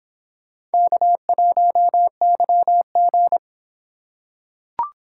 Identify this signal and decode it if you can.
Morse code, heard as a keyed tone, about 25 wpm, 700 Hz. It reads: K1YG